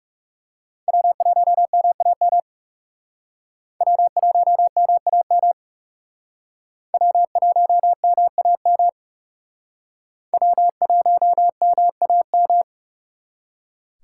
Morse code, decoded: W1MAM W1MAM W1MAM W1MAM

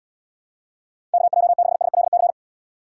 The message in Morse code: KJ6SLC

